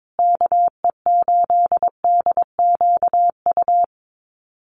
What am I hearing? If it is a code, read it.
Morse code, heard as a keyed tone, about 22 wpm, 700 Hz. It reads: KE8DQU
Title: USA Call Signs Repeated 3x 22wpm